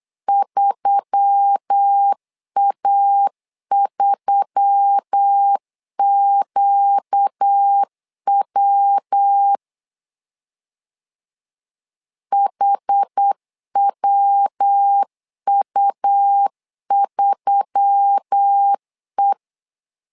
Morse code, heard as a keyed tone, about 8 wpm, 800 Hz. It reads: 3A3QW HWU3E
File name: LEZIONI / 4 Q1S3 8wpm 3037